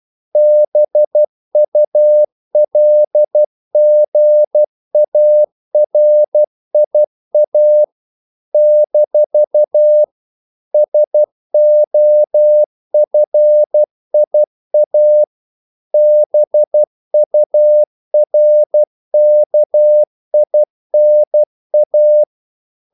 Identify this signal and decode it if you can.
Morse code, heard as a keyed tone, about 12 wpm, 600 Hz. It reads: BULGARIA - SOFIA BURKINA
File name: CW / Practice / NationalCapitals12